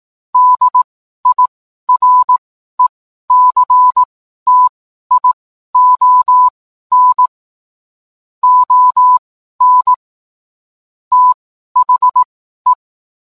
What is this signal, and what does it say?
Morse code, heard as a keyed tone, about 18 wpm, spaced as for 12 wpm, 1000 Hz. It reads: DIRECTION ON THE